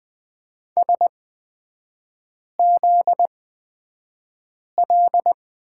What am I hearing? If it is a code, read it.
Morse code, heard as a keyed tone, about 20 wpm, spaced as for 5 wpm, 700 Hz. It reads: SZL
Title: 243 Lesson 39e - Letter z - Three Characters 20wpm